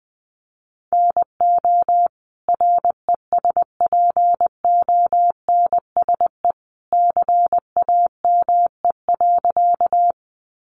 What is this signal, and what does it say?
Morse code, heard as a keyed tone, about 20 wpm, 700 Hz. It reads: NO RESPONSE CAME.
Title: Sentences from Top 1400 Words 20wpm